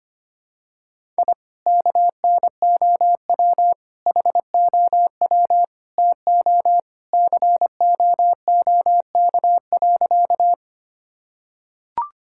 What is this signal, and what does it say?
Morse code, heard as a keyed tone, about 25 wpm, 700 Hz. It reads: I KNOW HOW TO COOK.
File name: practice / 25wpm / Sentences with 3-Word Phrases - 25wpm